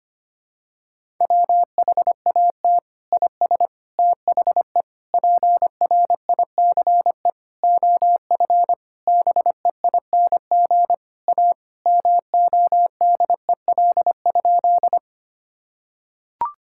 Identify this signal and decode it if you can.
Morse code, heard as a keyed tone, about 25 wpm, 700 Hz. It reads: WHAT IS THE PRICE OF BEING A MODEL?
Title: Sentences from Top 1000 Words - Review Part 5 - 25wpm